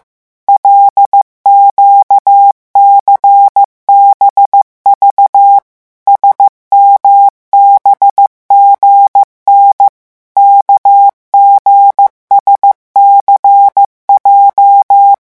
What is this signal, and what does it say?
Morse code, heard as a keyed tone, about 15 wpm, 800 Hz. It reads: LQCBV SMBGN KGSCJ
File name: morse mp3 / revision8-vitesse 14 mots